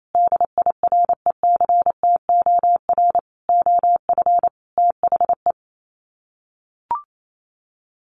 Morse code, heard as a keyed tone, about 28 wpm, 700 Hz. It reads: DIRECTOR OF THE
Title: Sets of 5 Words Encore 28wpm